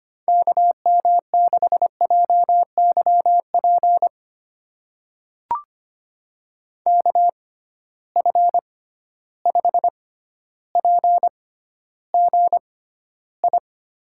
Morse code, heard as a keyed tone, about 25 wpm, spaced as for 13 wpm, 700 Hz. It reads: KM6JYP KF5PGI
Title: USA Call Signs - 6 Characters - 6x Character Spacing - 25wpm